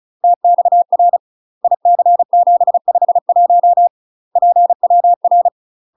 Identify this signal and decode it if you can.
Morse code, heard as a keyed tone, about 34 wpm, 700 Hz. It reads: TXR IC751 PWR
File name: UK MASTER QSOs TRAINING / 35WPM / UK MASTER QSOs 0002